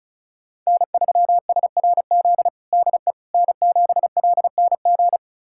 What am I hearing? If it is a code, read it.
Morse code, heard as a keyed tone, about 36 wpm, 700 Hz. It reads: N3SRZ DE N7LNG
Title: USA Callsign DE Callsign - 5 Characters - 35wpm